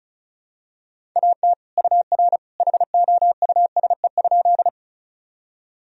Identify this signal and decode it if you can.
Morse code, heard as a keyed tone, about 36 wpm, 700 Hz. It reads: AT UR HOUSE?
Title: Sentences from Top 200 Words and Abbreviations 35wpm